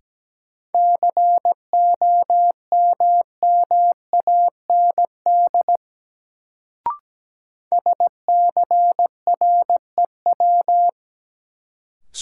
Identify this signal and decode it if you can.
Morse code, heard as a keyed tone, about 17 wpm, 700 Hz. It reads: COMMAND SCREW
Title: Sentences from Top 1600 Words - New Word Drill Alt - 17wpm